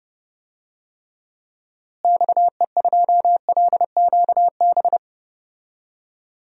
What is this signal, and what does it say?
Morse code, heard as a keyed tone, about 30 wpm, 700 Hz. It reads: XE2LQB
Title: DXCC-100 Call Signs - 6 Characters - Repeated 3x - 30wpm